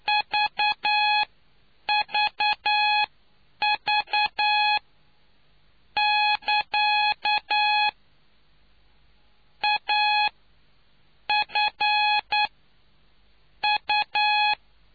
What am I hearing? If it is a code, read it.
Morse code, heard as a keyed tone, about 9 wpm, spaced as for 8 wpm, 840 Hz. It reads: VVV <KA> A F U